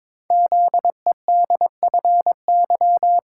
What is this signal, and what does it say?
Morse code, heard as a keyed tone, about 22 wpm, 700 Hz. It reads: ZEDFY
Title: Code Groups - 5 Letters - 22wpm